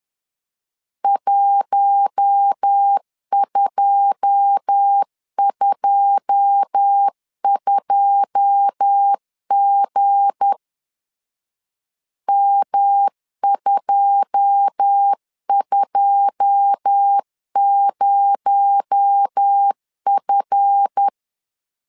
Morse code, heard as a keyed tone, about 10 wpm, 800 Hz. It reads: 1222G M220F
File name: LEZIONI / 9 2 10wpm 2736